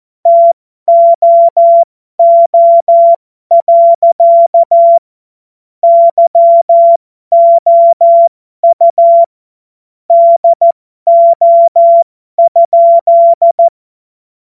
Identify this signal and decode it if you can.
Morse code, heard as a keyed tone, about 14 wpm, spaced as for 12 wpm, 680 Hz. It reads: TOO. YOU DO?
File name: Morsekurs / Witze